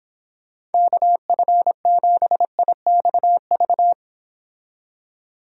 Morse code, heard as a keyed tone, about 30 wpm, 700 Hz. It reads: KF7IXV